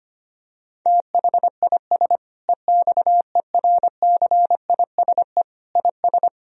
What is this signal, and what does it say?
Morse code, heard as a keyed tone, about 25 wpm, 700 Hz. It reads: THIS EXERCISE IS
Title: Sentences from Top 1000 Words - Review Part 4 - 25wpm